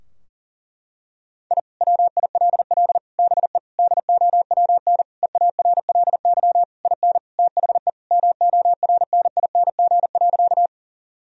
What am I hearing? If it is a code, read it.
Morse code, heard as a keyed tone, about 40 wpm, 700 Hz. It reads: I WILL BE DOWN EARLY IN THE MORNING.
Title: Sentences from Top 1000 Words - Review Part 2 - 40wpm